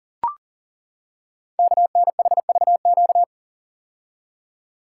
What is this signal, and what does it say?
Morse code, heard as a keyed tone, about 40 wpm, 700 Hz. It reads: KNHVQ
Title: Code Groups - 5 Letters - 40wpm